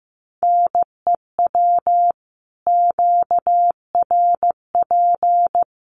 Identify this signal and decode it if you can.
Morse code, heard as a keyed tone, about 15 wpm, 700 Hz. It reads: NEW QRP
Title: Common QSO Phrases 15wpm